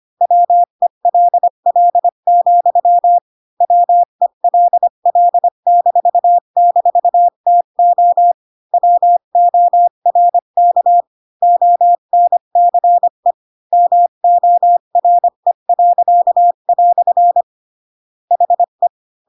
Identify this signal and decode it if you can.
Morse code, heard as a keyed tone, about 25 wpm, 700 Hz. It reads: WELL, WELL--TO WORK ONCE MORE." HE
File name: Pinocchio / Pinocchio ch1 20 25 ik1hge